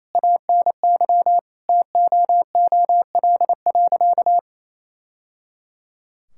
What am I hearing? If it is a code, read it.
Morse code, heard as a keyed tone, about 28 wpm, 700 Hz. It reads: ANY TOOL.
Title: Sentences from Top 1000 Words - Review Part 6 - 28wpm